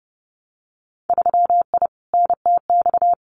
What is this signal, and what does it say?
Morse code, heard as a keyed tone, about 30 wpm, 700 Hz. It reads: 3I NTX
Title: Winter Field Day Exchanges 30wpm